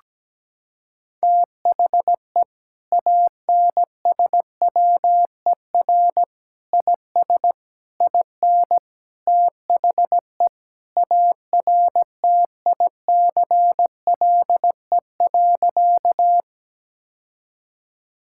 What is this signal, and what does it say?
Morse code, heard as a keyed tone, about 17 wpm, 700 Hz. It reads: THE ANSWER IS IN THE ARTICLE.
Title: Sentences from Top 1000 Words - Review Part 5 - 17wpm